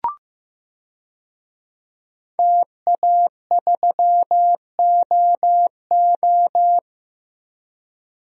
Morse code, heard as a keyed tone, about 16 wpm, 700 Hz. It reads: TA3OO
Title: DXCC-100 Call Signs - 5 Characters - 15wpm